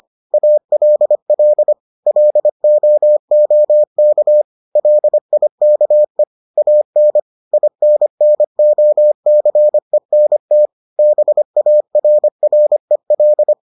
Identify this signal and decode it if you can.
Morse code, heard as a keyed tone, about 25 wpm, 600 Hz. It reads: ALL LOOK LIKE AN INNOCENT BARREL